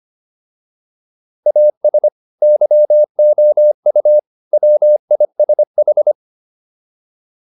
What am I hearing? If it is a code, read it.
Morse code, heard as a keyed tone, about 25 wpm, 600 Hz. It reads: AS YOU WISH
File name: PR302 25